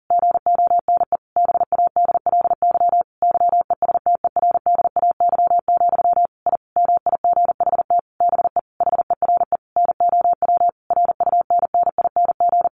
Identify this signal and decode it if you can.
Morse code, heard as a keyed tone, about 40 wpm, 700 Hz. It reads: GONE BADLY YESTERDAY, I MIGHT BE HERE NOW RUNNING
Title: A Sound Of Thunder 40wpm